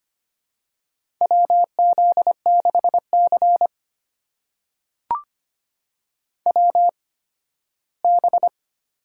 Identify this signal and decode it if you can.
Morse code, heard as a keyed tone, about 25 wpm, spaced as for 15 wpm, 700 Hz. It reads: WZ6C WB
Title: USA Call Signs - 4 Characters - 8x Character Spacing - 25wpm